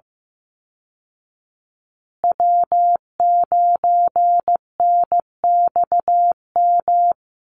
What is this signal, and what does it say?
Morse code, heard as a keyed tone, about 15 wpm, 700 Hz. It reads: W9NXM
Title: Call Signs Repeated 2x - 15wpm